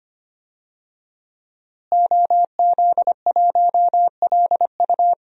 Morse code, heard as a keyed tone, about 25 wpm, 700 Hz. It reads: OZ1LU
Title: DXCC-100 Call Signs - 5 Characters - Repeated 2x - 25wpm